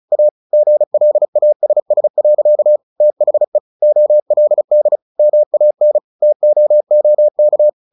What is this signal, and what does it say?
Morse code, heard as a keyed tone, about 36 wpm, 600 Hz. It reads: A GLASS. THE OLD MAN TOOK
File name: the old man and the sea 2 35wpm